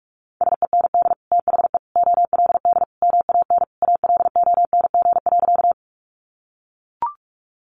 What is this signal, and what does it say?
Morse code, heard as a keyed tone, about 44 wpm, 700 Hz. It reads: SEND THE OLD MAN ALONG.
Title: Sentences from Top 1500 Words 45wpm